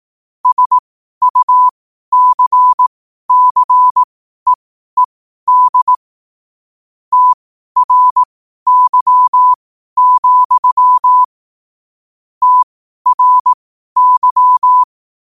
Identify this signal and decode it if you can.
Morse code, heard as a keyed tone, about 18 wpm, spaced as for 12 wpm, 1000 Hz. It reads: SUCCEED TRY, TRY
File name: qotdfiles / 12wpm